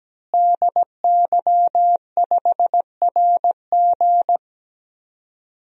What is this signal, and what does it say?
Morse code, heard as a keyed tone, about 17 wpm, 700 Hz. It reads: DY5RG
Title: DXCC-100 Call Signs - 5 Characters - Repeated 2x - 17wpm